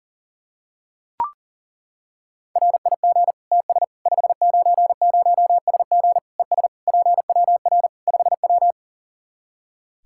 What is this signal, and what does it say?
Morse code, heard as a keyed tone, about 40 wpm, 700 Hz. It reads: RIG TS 590SG ES PWR 5W
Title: Practice QSO - Exchange 2 - Info - 40wpm